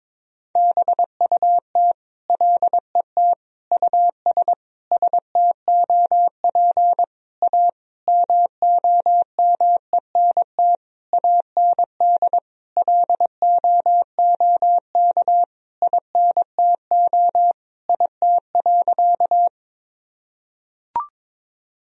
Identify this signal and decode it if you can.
Morse code, heard as a keyed tone, about 22 wpm, 700 Hz. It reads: BUT LET US STOP A MOMENT AND LOOK INTO IT.
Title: Sentences from Top 1000 Words - Review Part 1 - 22wpm